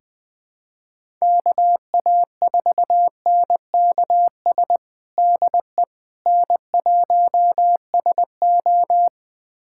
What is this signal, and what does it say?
Morse code, heard as a keyed tone, about 20 wpm, 700 Hz. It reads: KA4NKS DE N1SO
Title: USA Callsign DE Callsign 20wpm